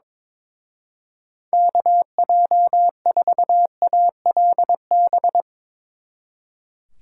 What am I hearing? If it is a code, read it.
Morse code, heard as a keyed tone, about 22 wpm, 700 Hz. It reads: KJ4ALB
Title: Speed Racing - USA 2x3 Call Signs 22wpm